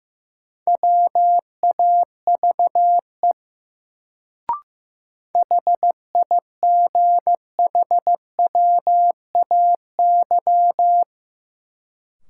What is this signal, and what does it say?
Morse code, heard as a keyed tone, about 15 wpm, 700 Hz. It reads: WAVE HIGHWAY
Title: Sentences from Top 1400 Words - New Word Drill Alt - 15wpm